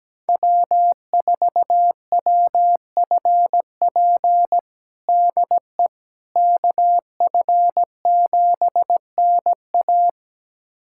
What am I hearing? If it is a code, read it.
Morse code, heard as a keyed tone, about 17 wpm, 700 Hz. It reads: W4WFP DE KF7NA
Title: USA Callsign DE Callsign - 5 Characters - 17wpm